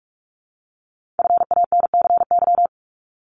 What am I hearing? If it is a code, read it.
Morse code, heard as a keyed tone, about 46 wpm, 700 Hz. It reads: FANCY
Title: Multi-letter Phonemes 45wpm